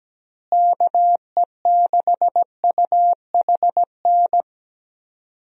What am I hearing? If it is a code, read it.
Morse code, heard as a keyed tone, about 17 wpm, 700 Hz. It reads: KE6UHN
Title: USA Call Signs - 6 Characters - Repeated 3x - 17wpm